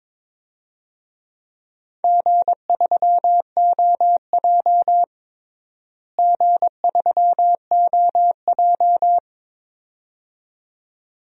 Morse code, heard as a keyed tone, about 22 wpm, 700 Hz. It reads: G3OJ G3OJ
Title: DXCC-100 Call Signs - 4 Characters - Repeated 2x - 22wpm